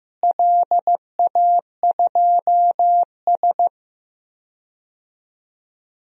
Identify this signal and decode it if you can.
Morse code, heard as a keyed tone, about 15 wpm, 700 Hz. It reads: LA2S
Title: DXCC-100 Call Signs - 4 Characters - Repeated 3x - 15wpm